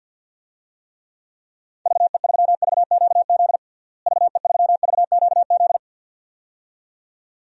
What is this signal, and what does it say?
Morse code, heard as a keyed tone, about 51 wpm, 700 Hz. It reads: VE3VQZ VE3VQZ
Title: DXCC-100 Call Signs - 6 Characters - Repeated 2x - 50wpm